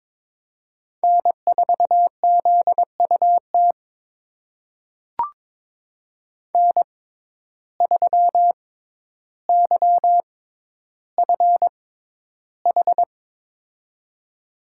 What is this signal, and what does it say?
Morse code, heard as a keyed tone, about 22 wpm, spaced as for 11 wpm, 700 Hz. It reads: N4ZUT N3YFH
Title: USA Call Signs - 5 Characters - 6x Character Spacing - 22wpm